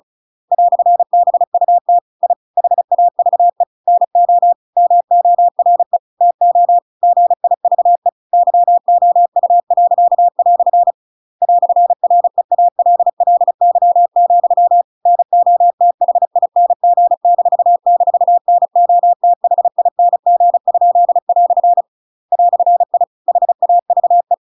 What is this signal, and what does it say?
Morse code, heard as a keyed tone, about 34 wpm, 700 Hz. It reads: "BUT I HAVE NO MORE TO GIVE YOU." "REALLY, NOTHING--NOTHING?" "I HAVE
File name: Pinocchio / Pinocchio ch7 35 35 ik1hge